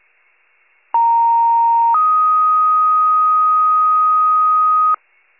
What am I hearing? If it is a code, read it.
Two-tone page, heard as a one-second tone, then a three-second tone, about 930 then 1250 Hz. It is heard from 0.9 to 4.9 s.